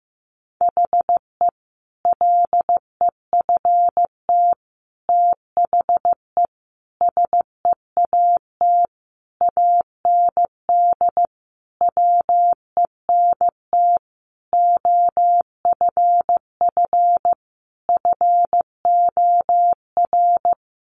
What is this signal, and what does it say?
Morse code, heard as a keyed tone, about 15 wpm, 700 Hz. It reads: HE LEFT THE SEAT AND WENT OFF FOR
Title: Sentences from Top 1500 Words 15wpm